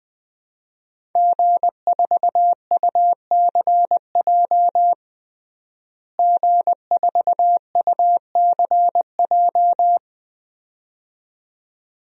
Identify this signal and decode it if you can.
Morse code, heard as a keyed tone, about 20 wpm, 700 Hz. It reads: G4UCJ G4UCJ
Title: DXCC-100 Call Signs - 5 Characters - Repeated 2x - 20wpm